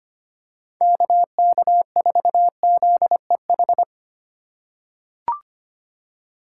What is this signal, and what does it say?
Morse code, heard as a keyed tone, about 25 wpm, 690 Hz. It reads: KK4ZEH
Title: USA Call Signs - 6 Characters - 3x Character Spacing - 25wpm